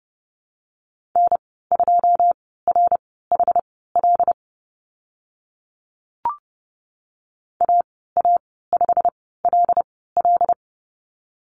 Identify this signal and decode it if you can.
Morse code, heard as a keyed tone, about 32 wpm, spaced as for 17 wpm, 700 Hz. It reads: N2RHL AA5LL
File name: Lesson 15g - Number 5 - Call Signs 30wpm